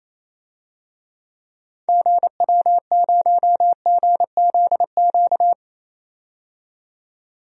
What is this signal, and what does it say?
Morse code, heard as a keyed tone, about 28 wpm, 700 Hz. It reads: GW0GZQ
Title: DXCC-100 Call Signs - 6 Characters - 28wpm